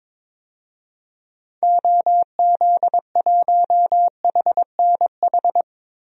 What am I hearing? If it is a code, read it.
Morse code, heard as a keyed tone, about 22 wpm, 700 Hz. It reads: OZ1HNH
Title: DXCC-100 Call Signs - 6 Characters - Repeated 3x - 22wpm